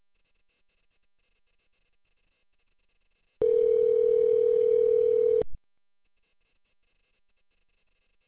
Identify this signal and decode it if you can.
Ringback tone (North America), 440 + 480 Hz.